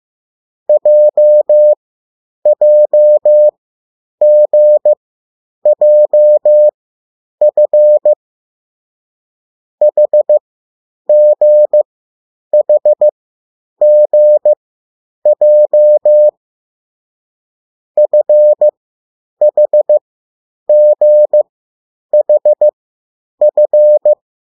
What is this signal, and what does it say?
Morse code, heard as a keyed tone, about 15 wpm, spaced as for 9 wpm, 600 Hz. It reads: JJGJF HGHGJ FHGHF
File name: urok 1